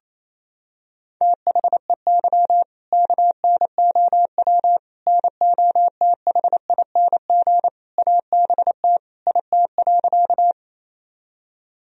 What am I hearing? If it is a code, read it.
Morse code, heard as a keyed tone, about 28 wpm, 700 Hz. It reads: THEY KNOW NOTHING ABT IT.
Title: Sentences from Top 500 Words and Abbreviations 28wpm